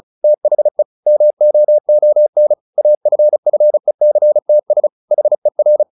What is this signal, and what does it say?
Morse code, heard as a keyed tone, about 34 wpm, 600 Hz. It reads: THE MOON AFFECTS HER
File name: the old man and the sea / the old man and the sea 1 35wpm